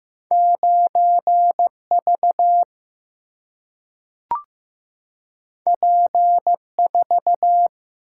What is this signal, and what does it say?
Morse code, heard as a keyed tone, about 15 wpm, 700 Hz. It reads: 9 V P 4
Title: Single-Letter Single-Number 15wpm